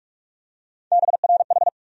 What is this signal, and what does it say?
Morse code, heard as a keyed tone, about 42 wpm, 700 Hz. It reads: BRH
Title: USA Call Sign Postfixes - 3 Letters 45wpm